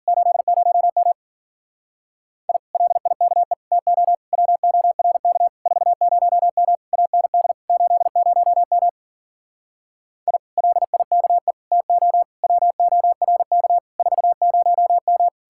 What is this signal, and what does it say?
Morse code, heard as a keyed tone, about 51 wpm, 700 Hz. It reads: 80M I LIKE TO WORK 40M AND 80M I LIKE TO WORK 40M